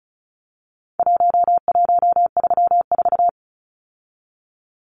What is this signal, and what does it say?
Morse code, heard as a keyed tone, about 34 wpm, 700 Hz. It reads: 1134
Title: Numbers - Contest Style - 4-Digit 35wpm